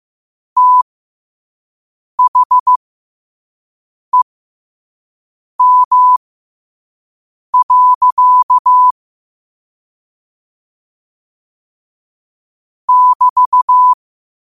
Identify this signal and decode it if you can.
Morse code, heard as a keyed tone, about 15 wpm, spaced as for 5 wpm, 1000 Hz. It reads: THEM. <BT>
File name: qotdfiles / 5wpm